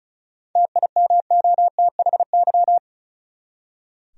Morse code, heard as a keyed tone, about 34 wpm, 700 Hz. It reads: TIMOTHY
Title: USA Common Names - 35wpm